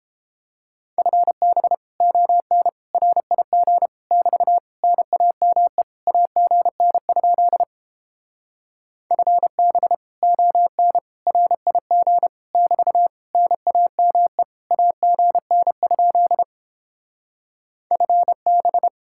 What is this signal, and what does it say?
Morse code, heard as a keyed tone, about 29 wpm, 700 Hz. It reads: FB ON RIG <BT> NAME AGN? FB ON RIG <BT> NAME AGN? FB